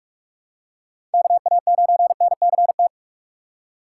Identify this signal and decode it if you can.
Morse code, heard as a keyed tone, about 46 wpm, 700 Hz. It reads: KA9NCT
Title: USA Call Signs - 6 Characters - 3x Character Spacing - 45wpm